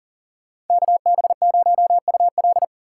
Morse code, heard as a keyed tone, about 53 wpm, 700 Hz. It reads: KB0UL